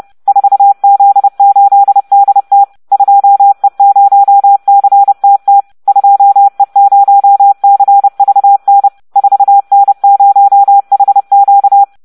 Morse code, heard as a keyed tone, about 30 wpm, 800 Hz. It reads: 4Z8DT 2E0CTT 2E0CVN 4N0HQ